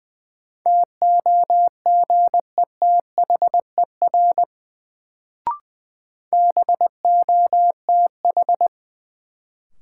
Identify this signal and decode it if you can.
Morse code, heard as a keyed tone, about 20 wpm, 700 Hz. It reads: TOGETHER BOTH
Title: Sentences from Top 300 Words - New Word Drill - 20wpm